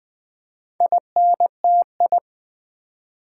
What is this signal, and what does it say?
Morse code, heard as a keyed tone, about 20 wpm, 700 Hz. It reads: INTI
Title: Build A Word - 2 Letters At A Time - 8-Letter Words - 20wpm